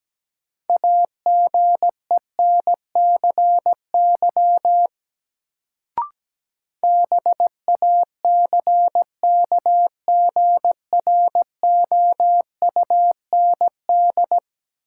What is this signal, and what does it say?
Morse code, heard as a keyed tone, about 17 wpm, 700 Hz. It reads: AGENCY BACKGROUND
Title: Sentences from Top 1300 Words - New Word Drill - 17wpm